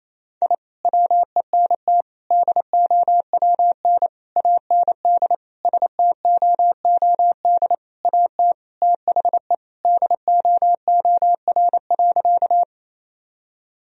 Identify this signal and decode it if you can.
Morse code, heard as a keyed tone, about 28 wpm, 700 Hz. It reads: I WENT DOWN AND STOOD AT THE DOOR.
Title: Sentences from Top 1000 Words - Review Part 2 - 28wpm